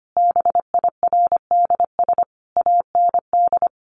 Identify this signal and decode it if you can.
Morse code, heard as a keyed tone, about 25 wpm, 700 Hz. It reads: BIRDS AND
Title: A Sound Of Thunder 25wpm